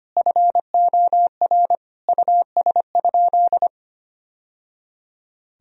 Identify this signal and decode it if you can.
Morse code, heard as a keyed tone, about 25 wpm, 700 Hz. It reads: FOR US?
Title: Sentences from Top 1000 Words - Review Part 3 - 25wpm